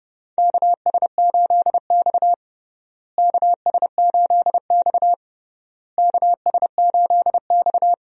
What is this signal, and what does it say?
Morse code, heard as a keyed tone, about 32 wpm, 700 Hz. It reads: KS8X KS8X KS8X
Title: USA Call Signs - 4 Characters - Repeated 3x - 30wpm